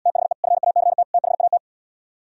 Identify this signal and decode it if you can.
Morse code, heard as a keyed tone, about 77 wpm, 690 Hz. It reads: THE FACT THAT